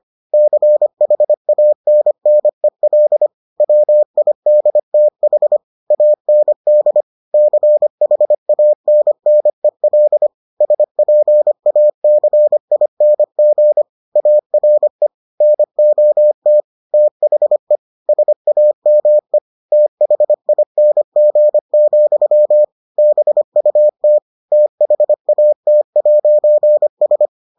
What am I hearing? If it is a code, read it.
Morse code, heard as a keyed tone, about 25 wpm, 600 Hz. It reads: CHANNEL WIDTH AND CHANNEL SPACING ARE NOT THE SAME THING, BUT THAT'S